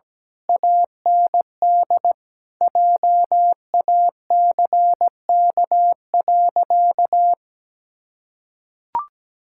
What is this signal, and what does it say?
Morse code, heard as a keyed tone, about 17 wpm, 700 Hz. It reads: AND JACK.
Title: Pangram Challenge v2 - 17wpm